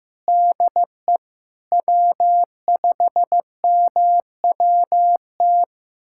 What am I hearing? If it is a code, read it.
Morse code, heard as a keyed tone, about 15 wpm, 700 Hz. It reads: DE W5MWT
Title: USA Callsign DE Callsign - 5 Characters - 15wpm